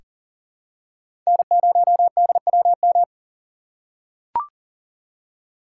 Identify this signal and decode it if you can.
Morse code, heard as a keyed tone, about 40 wpm, 700 Hz. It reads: N0DWM
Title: USA Call Signs - 5 Characters - 2x Character Spacing - 40wpm